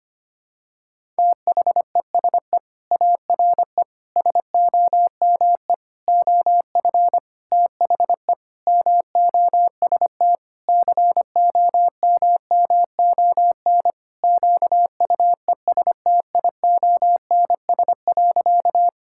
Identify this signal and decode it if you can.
Morse code, heard as a keyed tone, about 25 wpm, 700 Hz. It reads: THESE ARE SOME OF THE MOST COMMON QUESTIONS.
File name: practice / 25wpm / Sentences with 4-Word Phrases - 25wpm